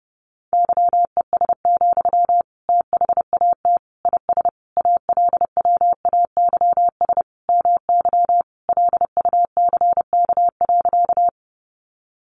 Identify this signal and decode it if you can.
Morse code, heard as a keyed tone, about 30 wpm, 700 Hz. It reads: YES, THAT IS ALWAYS MY LUCK.
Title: Sentences from Top 1500 Words 30wpm